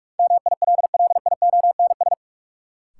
Morse code, heard as a keyed tone, about 43 wpm, 700 Hz. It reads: MILLIONS
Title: Sets of 5 Words - New Word Drill - 45wpm